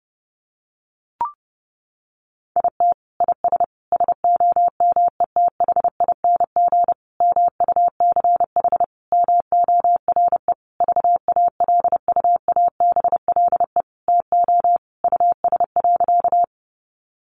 Morse code, heard as a keyed tone, about 28 wpm, 700 Hz. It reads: IT IS SOMETHING MUCH MORE VALUABLE TO US.
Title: Sentences from Top 1100 Words 30wpm